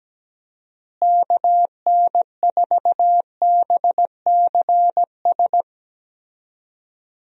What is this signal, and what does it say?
Morse code, heard as a keyed tone, about 17 wpm, 700 Hz. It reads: KN4BCS
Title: USA Call Signs - 6 Characters - Repeated 2x - 17wpm